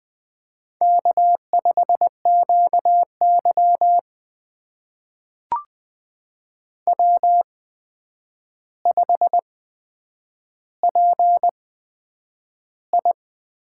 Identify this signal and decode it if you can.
Morse code, heard as a keyed tone, about 20 wpm, spaced as for 9 wpm, 700 Hz. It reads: K5QY W5PI
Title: USA Call Signs - 4 Characters - 8x Character Spacing - 20wpm